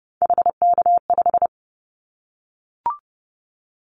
Morse code, heard as a keyed tone, about 30 wpm, 700 Hz. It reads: HK5
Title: Callsign Prefixes for Top 100 DXCC - 30wpm